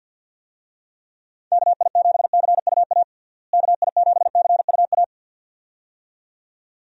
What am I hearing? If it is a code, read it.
Morse code, heard as a keyed tone, about 45 wpm, 700 Hz. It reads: KI7CUA KI7CUA
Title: USA Call Signs - 6 Characters - Repeated 2x - 50wpm